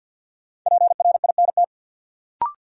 Morse code, heard as a keyed tone, about 48 wpm, 700 Hz. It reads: PRINT
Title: Sentences from Top 1400 Words - New Word Drill - 50wpm